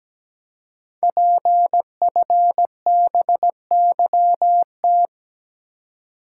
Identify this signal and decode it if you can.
Morse code, heard as a keyed tone, about 17 wpm, 700 Hz. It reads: PFBYT
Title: Code Groups - 5 Letters - 17wpm